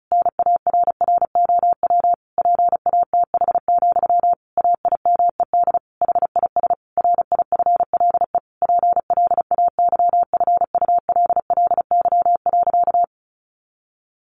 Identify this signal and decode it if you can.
Morse code, heard as a keyed tone, about 34 wpm, 700 Hz. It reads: NARROW PATH, AIMED HIS RIFLE PLAYFULLY.
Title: A Sound Of Thunder 35wpm